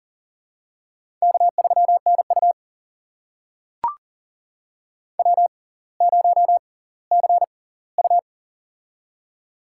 Morse code, heard as a keyed tone, about 40 wpm, spaced as for 20 wpm, 700 Hz. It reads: K3NU W0CU
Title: USA Call Signs - 4 Characters - 6x Character Spacing - 40wpm